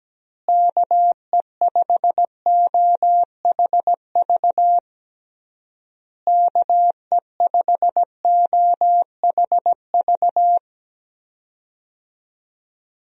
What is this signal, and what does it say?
Morse code, heard as a keyed tone, about 17 wpm, 700 Hz. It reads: KE5OHV KE5OHV
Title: USA Call Signs - 6 Characters - Repeated 3x - 17wpm